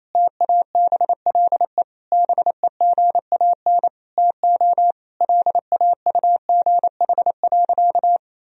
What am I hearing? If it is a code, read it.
Morse code, heard as a keyed tone, about 28 wpm, 700 Hz. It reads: TABLE BEGAN TO LAUGH.
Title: Sentences from Top 1000 Words - Review Part 2 - 28wpm